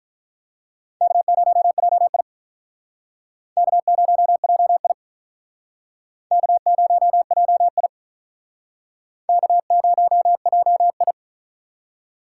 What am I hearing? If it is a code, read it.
Morse code, heard as a keyed tone, about 44 wpm, 700 Hz. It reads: K0JI K0JI K0JI K0JI
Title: Speed Racing - USA 1x2 Call Signs 35wpm